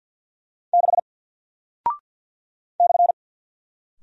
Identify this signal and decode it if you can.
Morse code, heard as a keyed tone, about 48 wpm, 700 Hz. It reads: 6 /
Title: Single Character (Letter:Number:Punctuation:Prosign) Alt 50wpm